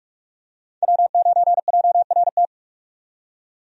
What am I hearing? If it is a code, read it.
Morse code, heard as a keyed tone, about 46 wpm, 700 Hz. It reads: W9JRT
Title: USA Call Signs - 5 Characters - 45wpm